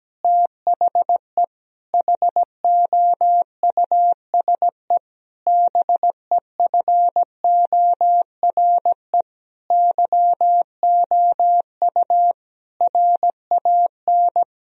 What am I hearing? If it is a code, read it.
Morse code, heard as a keyed tone, about 17 wpm, 700 Hz. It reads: THE HOUSE BEFORE YOU RAN
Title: Sentences from Top 1000 Words - Review Part 3 - 17wpm